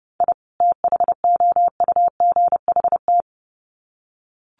Morse code, heard as a keyed tone, about 29 wpm, 700 Hz. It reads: I THOUGHT
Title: Sets of 2 Words 30wpm